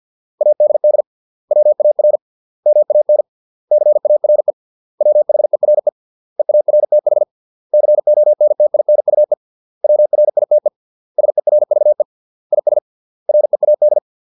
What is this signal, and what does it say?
Morse code, heard as a keyed tone, about 50 wpm, 600 Hz. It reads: ADD WAR MAN CARE WHERE EARTH CONTINUE WRITE SERVE IS READ